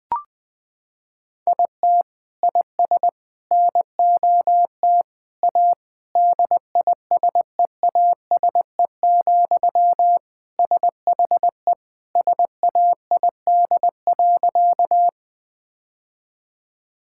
Morse code, heard as a keyed tone, about 19 wpm, 700 Hz. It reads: IT IS NOT A DISEASE, SHE SAID.
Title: Sentences from Top 1000 Words - Review Part 6 - 20wpm